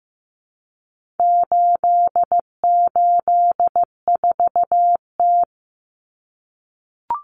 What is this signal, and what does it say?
Morse code, heard as a keyed tone, about 15 wpm, 700 Hz. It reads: 884T
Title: Common Cut Numbers - 4-Digit 15wpm